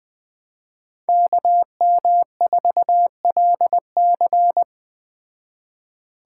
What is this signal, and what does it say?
Morse code, heard as a keyed tone, about 20 wpm, 700 Hz. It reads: KM4LC